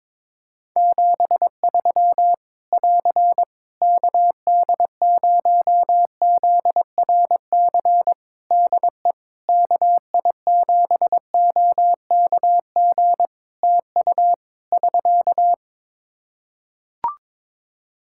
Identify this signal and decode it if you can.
Morse code, heard as a keyed tone, about 22 wpm, 700 Hz. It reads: 73 <AR> KD0ZRC DE KI7OKG TU <SK>
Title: K1USN SST QSO Practice 22wpm